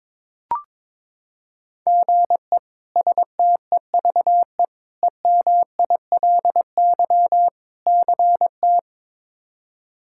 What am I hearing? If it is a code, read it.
Morse code, heard as a keyed tone, about 21 wpm, 700 Hz. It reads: GE STEVE EMILY CT
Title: K1USN SST QSO Practice - 22wpm